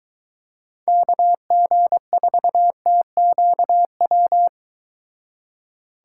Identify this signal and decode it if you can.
Morse code, heard as a keyed tone, about 23 wpm, 700 Hz. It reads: KG4TQW